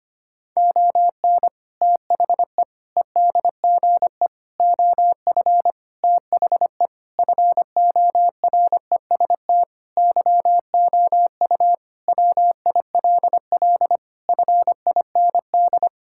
Morse code, heard as a keyed tone, about 25 wpm, 700 Hz. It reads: ON THE EDGE OF THE FOREST YOU WILL FIND
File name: practice / 25wpm / Sentences with 4-Word Phrases - 25wpm